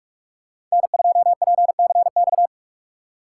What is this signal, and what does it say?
Morse code, heard as a keyed tone, about 46 wpm, 700 Hz. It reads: N2PCX